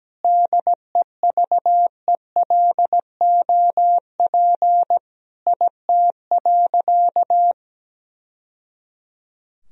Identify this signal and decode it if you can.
Morse code, heard as a keyed tone, about 17 wpm, 700 Hz. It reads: DEVELOP IT.
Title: Sentences from Top 1000 Words - Review Part 2 - 17wpm